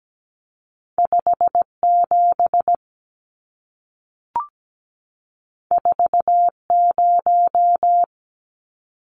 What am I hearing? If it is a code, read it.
Morse code, heard as a keyed tone, about 17 wpm, 700 Hz. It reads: 57 40
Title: Numbers - 2-Digit 17wpm